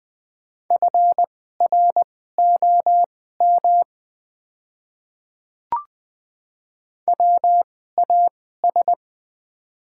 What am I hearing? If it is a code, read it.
Morse code, heard as a keyed tone, about 20 wpm, spaced as for 14 wpm, 700 Hz. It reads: FROM WAS
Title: 2x Character Spacing - Top 100 Words - 20wpm